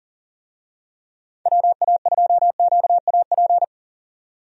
Morse code, heard as a keyed tone, about 39 wpm, 700 Hz. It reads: WA2QAP
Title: USA Call Signs - 6 Characters - 40wpm